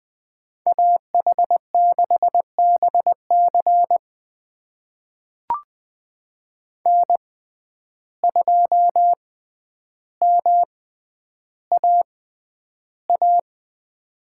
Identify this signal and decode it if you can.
Morse code, heard as a keyed tone, about 21 wpm, spaced as for 10 wpm, 700 Hz. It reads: AH6BC N2MAA
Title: USA Call Signs - 5 Characters - 6x Character Spacing - 20wpm